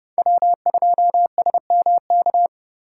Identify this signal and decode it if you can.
Morse code, heard as a keyed tone, about 32 wpm, 700 Hz. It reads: W2SMK
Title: USA Call Signs - 3x Character Spacing - 30wpm